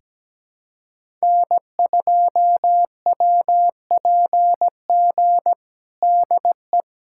Morse code, heard as a keyed tone, about 17 wpm, 700 Hz. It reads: N2WPG DE
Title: USA Callsign DE Callsign 17wpm